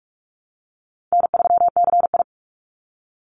Morse code, heard as a keyed tone, about 44 wpm, 700 Hz. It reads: N3CI
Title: Call Signs - 45wpm